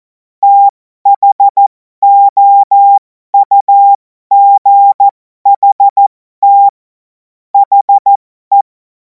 Morse code, heard as a keyed tone, about 14 wpm, spaced as for 12 wpm, 800 Hz. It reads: THOUGHT HE
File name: Morsekurs / Witze